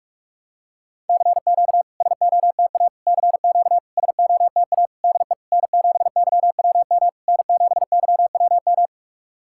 Morse code, heard as a keyed tone, about 44 wpm, 690 Hz. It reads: CQ SOTA CQ SOTA DE N7YWM N7YWM
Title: SOTA QSO Practice 45wpm